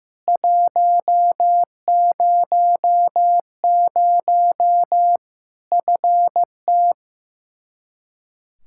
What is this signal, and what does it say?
Morse code, heard as a keyed tone, about 16 wpm, 700 Hz. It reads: 100 FT
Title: Practice QSO - Exchange 2 - Info - 15wpm